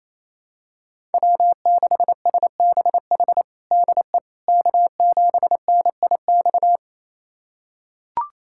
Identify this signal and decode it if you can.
Morse code, heard as a keyed tone, about 28 wpm, 700 Hz. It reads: W6SBH DE K7NIX
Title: USA Callsign DE Callsign 28wpm